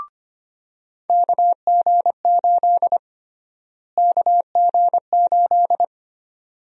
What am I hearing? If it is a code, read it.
Morse code, heard as a keyed tone, about 25 wpm, 700 Hz. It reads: KG8 KG8
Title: USA Call Sign Prefixes - Repeated 2x - 25wpm